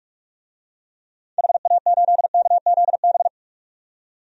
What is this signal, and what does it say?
Morse code, heard as a keyed tone, about 46 wpm, 700 Hz. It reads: HA8KZB
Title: DXCC-100 Call Signs - 6 Characters - 45wpm